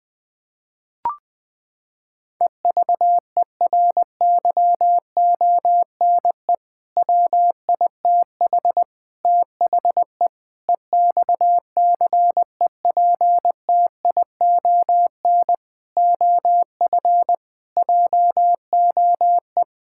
Morse code, heard as a keyed tone, about 21 wpm, 700 Hz. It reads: EVERYONE WITH THE EXCEPTION OF JOE
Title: Sentences with 4-Word Phrases - 20wpm